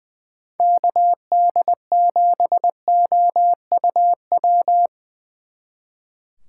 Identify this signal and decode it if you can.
Morse code, heard as a keyed tone, about 22 wpm, 700 Hz. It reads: KD7OUW
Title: Speed Racing - USA 2x3 Call Signs 20wpm